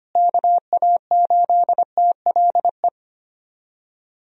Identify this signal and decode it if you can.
Morse code, heard as a keyed tone, about 25 wpm, 700 Hz. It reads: KA8TLE
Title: USA Call Signs - 6 Characters - Repeated 2x - 25wpm